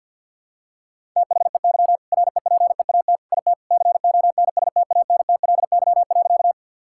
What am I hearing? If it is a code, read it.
Morse code, heard as a keyed tone, about 50 wpm, 700 Hz. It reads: THEY REPEAT IT CONSTANTLY.
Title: Sentences from Top 1600 Words 50wpm